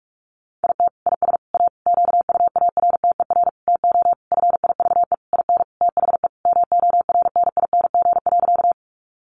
Morse code, heard as a keyed tone, about 44 wpm, 700 Hz. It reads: IT IS A QUARTER TO FIVE IN THE MORNING.
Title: Sentences from Top 1300 Words 45wpm